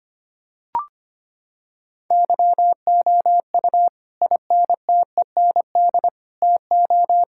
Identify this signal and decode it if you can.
Morse code, heard as a keyed tone, about 25 wpm, 700 Hz. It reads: YOU INTEND TO
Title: Sentences from Top 1600 Words 25wpm